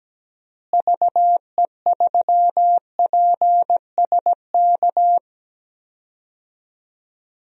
Morse code, heard as a keyed tone, about 17 wpm, 700 Hz. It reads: VE3PSK
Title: DXCC-100 Call Signs - 6 Characters - Repeated 3x - 17wpm